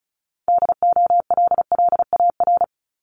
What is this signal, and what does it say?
Morse code, heard as a keyed tone, about 36 wpm, 700 Hz. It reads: DOLLAR
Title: Multi-letter Phonemes Repeated 3x 35wpm